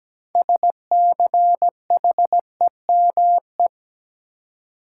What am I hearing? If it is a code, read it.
Morse code, heard as a keyed tone, about 17 wpm, 700 Hz. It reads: SCHEME
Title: Six Letter Words - 17wpm